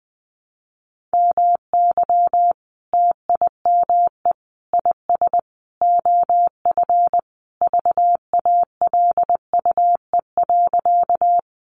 Morse code, heard as a keyed tone, about 19 wpm, 700 Hz. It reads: MY TIME IS OF VALUE.
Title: Sentences from Top 600 Words 20wpm